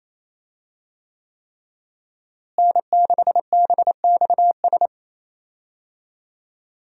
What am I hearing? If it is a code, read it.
Morse code, heard as a keyed tone, about 28 wpm, 700 Hz. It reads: N6BXS
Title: USA Call Signs - 5 Characters - Repeated 3x - 28wpm